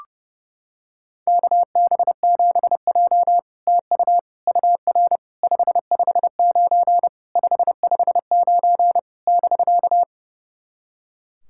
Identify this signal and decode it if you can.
Morse code, heard as a keyed tone, about 30 wpm, 700 Hz. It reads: KB7J TU UR 559 559 <BK>
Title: SOTA QSO Practice 30wpm